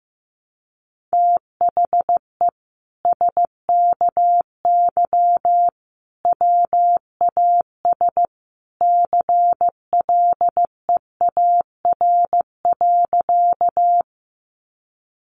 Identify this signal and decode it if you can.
Morse code, heard as a keyed tone, about 15 wpm, 700 Hz. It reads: THE SKY WAS CLEAR.
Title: Sentences from Top 1200 Words 15wpm